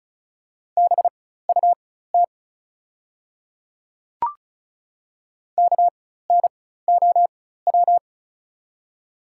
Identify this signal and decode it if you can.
Morse code, heard as a keyed tone, about 34 wpm, spaced as for 16 wpm, 690 Hz. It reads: BUT KNOW
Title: 4x Character Spacing - Top 500 Words - 35wpm